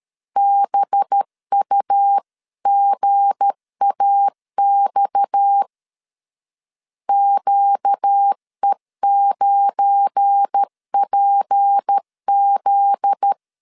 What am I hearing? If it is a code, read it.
Morse code, heard as a keyed tone, about 13 wpm, 800 Hz. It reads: BUGAX QE9PZ
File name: LEZIONI / 8 YJ45 / 8 YJ45 12wpm 2151